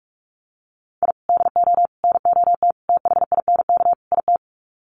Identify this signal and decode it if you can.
Morse code, heard as a keyed tone, about 44 wpm, 700 Hz. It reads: I DO NOT THINK IT
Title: Sets of 5 Words 45wpm